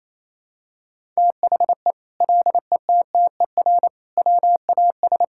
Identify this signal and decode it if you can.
Morse code, heard as a keyed tone, about 27 wpm, 700 Hz. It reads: THE LETTER WAS
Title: Sentences from Top 1000 Words - Review Part 5 - 28wpm